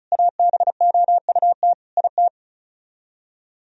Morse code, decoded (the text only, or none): ABOUT IT